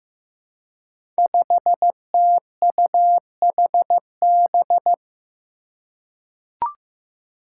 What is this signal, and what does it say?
Morse code, heard as a keyed tone, about 15 wpm, 700 Hz. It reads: 5TUHB
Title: Five Challenging Letter-Number Combinations 15wpm